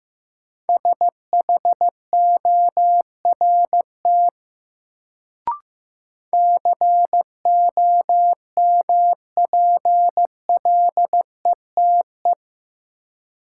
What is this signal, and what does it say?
Morse code, heard as a keyed tone, about 15 wpm, 700 Hz. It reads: SHORT COMPLETE
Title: Sentences from Top 400 Words - New Word Drill - 15wpm